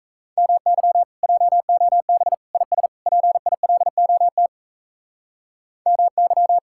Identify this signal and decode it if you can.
Morse code, heard as a keyed tone, about 34 wpm, 700 Hz. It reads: MY JOB IS PILOT MY
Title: Speed-Racing Common QSO Phrases Encore - 28wpm